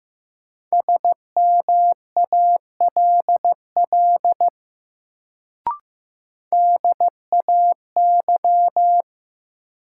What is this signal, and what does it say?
Morse code, heard as a keyed tone, about 15 wpm, 700 Hz. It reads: SMALL DAY
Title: Sentences from Top 200 Words - New Word Drill - 15wpm